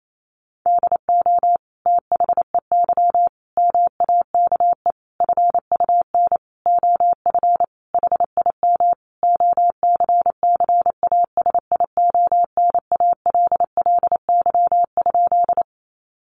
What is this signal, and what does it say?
Morse code, heard as a keyed tone, about 28 wpm, 700 Hz. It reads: DO THEY MAKE FUN OF HIM OCCASIONALLY?
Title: Sentences from Top 1300 Words 28wpm